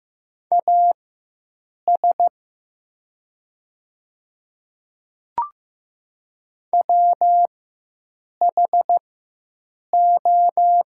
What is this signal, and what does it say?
Morse code, heard as a keyed tone, about 15 wpm, spaced as for 7 wpm, 700 Hz. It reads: AS WHO
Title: 4x Character Spacing - Top 100 Words - 15wpm